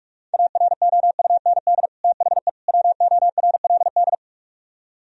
Morse code, heard as a keyed tone, about 45 wpm, 700 Hz. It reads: AROUND THE WORLD
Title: Speed-Racing - Sets of 3 Words 45wpm